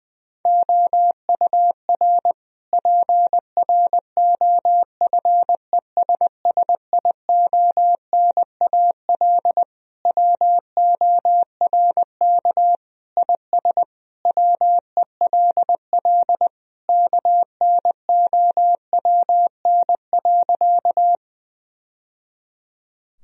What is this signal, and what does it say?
Morse code, heard as a keyed tone, about 21 wpm, 700 Hz. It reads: OUR PROFESSIONAL WORK IS WELL KNOWN.
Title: Sentences from Top 1000 Words - Review Part 4 - 20wpm